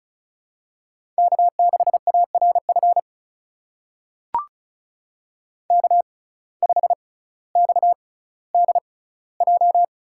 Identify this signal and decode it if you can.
Morse code, heard as a keyed tone, about 36 wpm, spaced as for 18 wpm, 700 Hz. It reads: K6ARF K5XDJ
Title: USA Call Signs - 6x Character Spacing - 35wpm